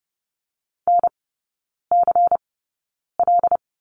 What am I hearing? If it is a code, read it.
Morse code, heard as a keyed tone, about 30 wpm, spaced as for 9 wpm, 690 Hz. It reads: NCL
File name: Lesson 17e - Letter u - Three Characters 30wpm